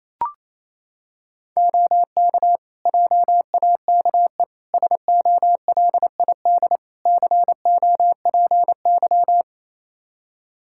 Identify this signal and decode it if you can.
Morse code, heard as a keyed tone, about 28 wpm, 700 Hz. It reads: OK JAKE SOLID COPY
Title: Practice QSO - Exchange 3 - More Info - 28wpm